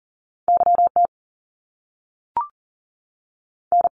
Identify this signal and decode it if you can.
Morse code, heard as a keyed tone, about 41 wpm, spaced as for 6 wpm, 700 Hz. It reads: YTN
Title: Canadian Provinces and Territories 40wpm